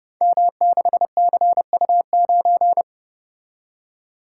Morse code, heard as a keyed tone, about 29 wpm, 700 Hz. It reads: M6CU9
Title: Code Groups - 5 Letter:Numbers - 30wpm